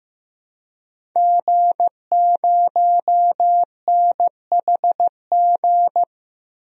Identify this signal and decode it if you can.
Morse code, heard as a keyed tone, about 15 wpm, 700 Hz. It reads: G0NHG
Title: DXCC-100 Call Signs - 5 Characters - Repeated 2x - 15wpm